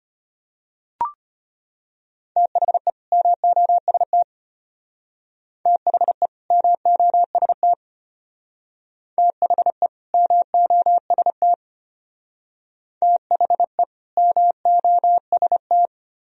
Morse code, heard as a keyed tone, about 32 wpm, 700 Hz. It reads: THE MOST THE MOST THE MOST THE MOST